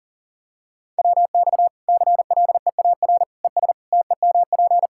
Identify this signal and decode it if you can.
Morse code, heard as a keyed tone, about 40 wpm, 700 Hz. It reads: WX CLEAR ES TEMP
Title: Practice QSO - Exchange 2 - Info - 40wpm